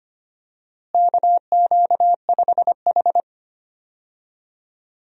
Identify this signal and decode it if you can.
Morse code, heard as a keyed tone, about 25 wpm, 700 Hz. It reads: KQ5H
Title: USA Call Signs - 4 Characters - Repeated 3x - 25wpm